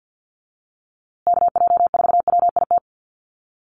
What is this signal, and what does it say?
Morse code, heard as a keyed tone, about 48 wpm, 700 Hz. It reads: KJ4WIT
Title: USA Call Signs Repeated 3x 50wpm